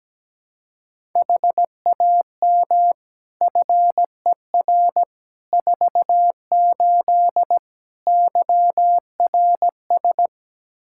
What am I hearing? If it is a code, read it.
Morse code, heard as a keyed tone, about 17 wpm, 700 Hz. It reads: HAM FER 48 YRS
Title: Practice QSO - Exchange 3 - More Info - 17wpm